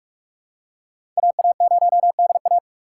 Morse code, heard as a keyed tone, about 42 wpm, 700 Hz. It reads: AA0DA
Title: USA Call Signs - 5 Characters - 4x Character Spacing - 45wpm